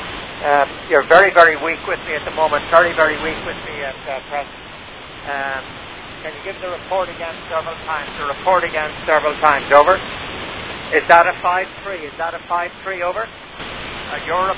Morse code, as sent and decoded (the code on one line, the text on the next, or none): none